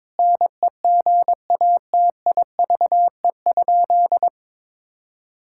Morse code, heard as a keyed tone, about 22 wpm, 700 Hz. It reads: NEGATIVE?
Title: Sentences from Top 1000 Words - Review Part 6 - 22wpm